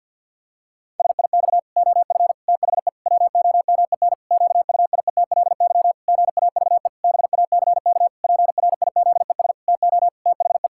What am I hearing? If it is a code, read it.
Morse code, heard as a keyed tone, about 50 wpm, 700 Hz. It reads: SIX OF THE WOMEN QUIETLY GAVE BACK PRIZES TO THE
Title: Pangram Challenge v2 - 50wpm